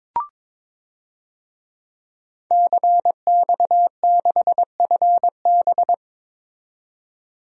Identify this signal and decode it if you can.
Morse code, heard as a keyed tone, about 22 wpm, 700 Hz. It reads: CX6FB
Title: DXCC-100 Call Signs - 5 Characters - 22wpm